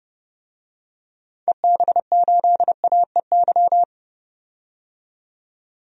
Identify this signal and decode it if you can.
Morse code, heard as a keyed tone, about 30 wpm, 700 Hz. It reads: EB8AEY
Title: DXCC-100 Call Signs - 6 Characters - 30wpm